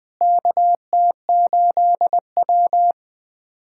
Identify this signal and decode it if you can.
Morse code, heard as a keyed tone, about 20 wpm, 700 Hz. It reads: KT8W
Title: USA Callsign DE Callsign - 4 Characters - 20wpm